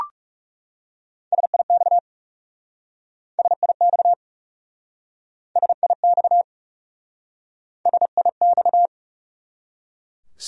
Morse code, heard as a keyed tone, about 37 wpm, 700 Hz. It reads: SIX SIX SIX SIX
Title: Speed-Racing Top 500 Words 30wpm